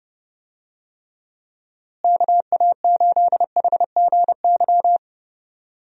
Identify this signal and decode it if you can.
Morse code, heard as a keyed tone, about 30 wpm, 700 Hz. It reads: KA8HGY
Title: USA Call Signs - 6 Characters - Repeated 2x - 30wpm